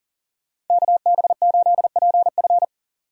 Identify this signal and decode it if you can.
Morse code, heard as a keyed tone, about 48 wpm, 700 Hz. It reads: KB8PF
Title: Speed Racing - USA 2x2 Call Signs 40wpm